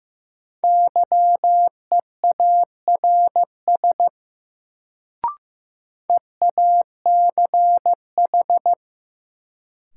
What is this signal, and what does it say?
Morse code, heard as a keyed tone, about 15 wpm, 700 Hz. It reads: YEARS EACH
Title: Sets of 2 Words - New Word Drill - 15wpm